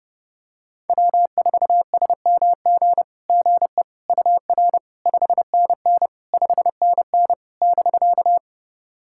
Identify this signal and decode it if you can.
Morse code, heard as a keyed tone, about 30 wpm, 700 Hz. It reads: W4SMG GE UR 5NN 5NN <BK>
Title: SOTA QSO Practice 30wpm